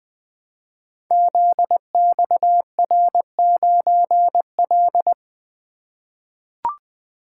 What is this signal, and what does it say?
Morse code, heard as a keyed tone, about 20 wpm, 700 Hz. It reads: ZXR9L
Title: Code Groups - 5 Letter:Numbers - 20wpm